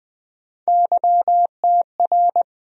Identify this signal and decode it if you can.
Morse code, heard as a keyed tone, about 20 wpm, 700 Hz. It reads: YTR
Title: USA Call Sign Postfixes - 3 Letters 20wpm